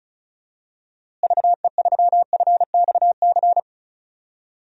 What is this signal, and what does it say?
Morse code, heard as a keyed tone, about 36 wpm, 700 Hz. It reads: VE3FXC
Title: DXCC-100 Call Signs - 6 Characters - Repeated 3x - 35wpm